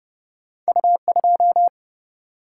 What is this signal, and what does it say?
Morse code, heard as a keyed tone, about 30 wpm, 700 Hz. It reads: U 2
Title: Single-Letter Single-Number 30wpm